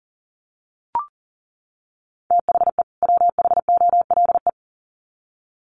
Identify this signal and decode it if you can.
Morse code, heard as a keyed tone, about 40 wpm, 700 Hz. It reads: THE WHOLE
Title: Sets of 2 Words 40wpm